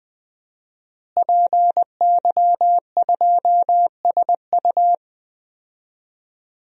Morse code, heard as a keyed tone, about 21 wpm, 700 Hz. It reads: PY2SU
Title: DXCC-100 Call Signs - 5 Characters - Repeated 2x - 20wpm